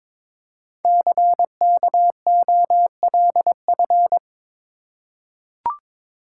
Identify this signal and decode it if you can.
Morse code, heard as a keyed tone, about 22 wpm, 700 Hz. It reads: CKOLF
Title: Code Groups - 5 Letters - 22wpm